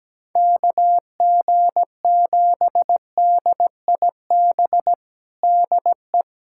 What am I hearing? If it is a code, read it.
Morse code, heard as a keyed tone, about 17 wpm, 700 Hz. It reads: KG7DIB DE
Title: USA Callsign DE Callsign 17wpm